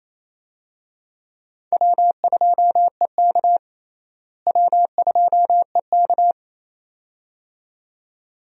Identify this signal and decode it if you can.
Morse code, heard as a keyed tone, about 28 wpm, 700 Hz. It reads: W2EK W2EK
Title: USA Call Signs - 4 Characters - Repeated 2x - 28wpm